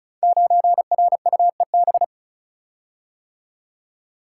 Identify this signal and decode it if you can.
Morse code, heard as a keyed tone, about 36 wpm, 700 Hz. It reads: 9RUEB
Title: Code Groups - 5 Letter:Numbers - 35wpm